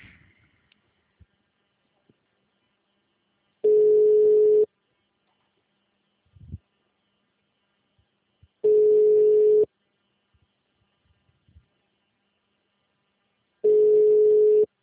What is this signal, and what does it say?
Ringback tone (Europe, 425 Hz).